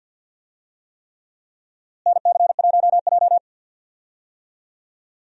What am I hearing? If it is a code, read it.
Morse code, heard as a keyed tone, about 50 wpm, 700 Hz. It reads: NC1J
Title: USA Call Signs - 4 Characters - Repeated 3x - 50wpm